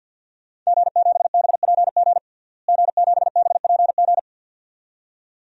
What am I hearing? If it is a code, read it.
Morse code, heard as a keyed tone, about 49 wpm, 700 Hz. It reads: G7BPG G7BPG
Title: DXCC-100 Call Signs - 5 Characters - Repeated 2x - 50wpm